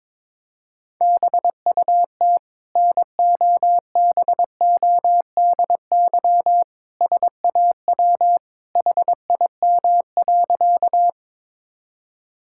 Morse code, heard as a keyed tone, about 22 wpm, 700 Hz. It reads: BUT NOBODY SAW HIM.
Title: Sentences from Top 1600 Words 22wpm